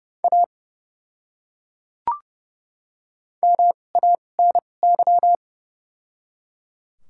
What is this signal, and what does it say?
Morse code, heard as a keyed tone, about 30 wpm, spaced as for 22 wpm, 700 Hz. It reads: A MANY
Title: 2x Character Spacing - Top 100 Words - 30wpm